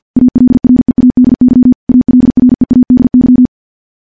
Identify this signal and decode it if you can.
Morse code, heard as a keyed tone, about 50 wpm, 250 Hz. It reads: ALREADY ALREADY
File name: morse / morse-training-50WPM